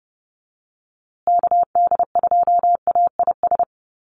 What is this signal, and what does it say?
Morse code, heard as a keyed tone, about 30 wpm, 700 Hz. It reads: KD2AIS
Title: USA Call Signs Repeated 2x 30wpm